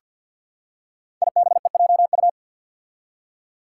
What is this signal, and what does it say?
Morse code, heard as a keyed tone, about 48 wpm, 700 Hz. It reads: IBEJU
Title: Five Challenging Letter-Number Combinations 50wpm